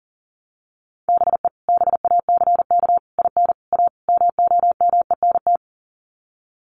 Morse code, heard as a keyed tone, about 40 wpm, 700 Hz. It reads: BE BACK IN A MOMENT
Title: Sets of 5 Words 40wpm